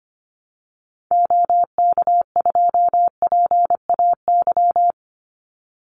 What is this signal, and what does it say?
Morse code, heard as a keyed tone, about 25 wpm, 700 Hz. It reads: OK2PAY
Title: Call Signs Repeated 3x - 25wpm